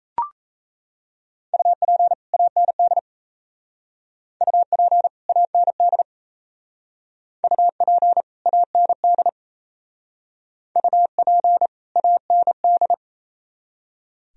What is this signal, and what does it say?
Morse code, heard as a keyed tone, about 36 wpm, 700 Hz. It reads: UP AND UP AND UP AND UP AND